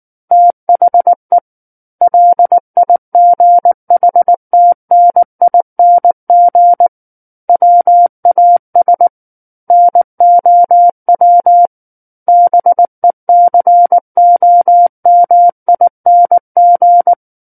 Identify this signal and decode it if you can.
Morse code, heard as a keyed tone, about 19 wpm, 700 Hz. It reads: THE LIGHTNING WAS NOW BECOMING
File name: Book One / 10-Chapter 10 - 19 WPM